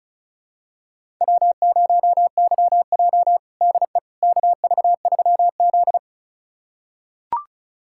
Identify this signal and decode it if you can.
Morse code, heard as a keyed tone, about 36 wpm, 700 Hz. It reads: W0YJ DE KV3Z
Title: USA Callsign DE Callsign - 4 Characters - 35wpm